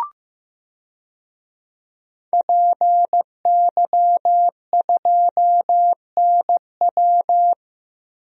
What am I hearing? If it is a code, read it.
Morse code, heard as a keyed tone, about 15 wpm, 700 Hz. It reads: PY2NW
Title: DXCC-100 Call Signs - 5 Characters - 15wpm